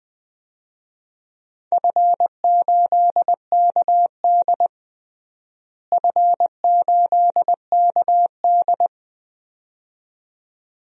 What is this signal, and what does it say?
Morse code, heard as a keyed tone, about 20 wpm, 700 Hz. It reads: F8KD F8KD
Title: DXCC-100 Call Signs - 4 Characters - Repeated 2x - 20wpm